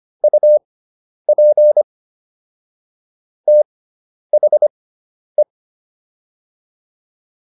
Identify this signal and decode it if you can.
Morse code, heard as a keyed tone, about 25 wpm, spaced as for 10 wpm, 600 Hz. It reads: UP THE